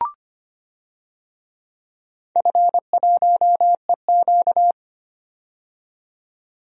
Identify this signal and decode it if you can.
Morse code, heard as a keyed tone, about 25 wpm, 700 Hz. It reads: F1EQ
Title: DXCC-100 Call Signs - 4 Characters - 25wpm